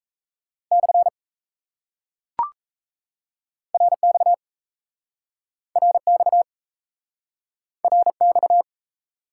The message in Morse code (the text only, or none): / RX RX RX